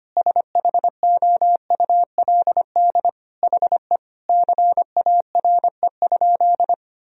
Morse code, heard as a keyed tone, about 25 wpm, 700 Hz. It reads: SHOULD HE CARE?
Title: Sentences from Top 1000 Words - Review Part 1 - 25wpm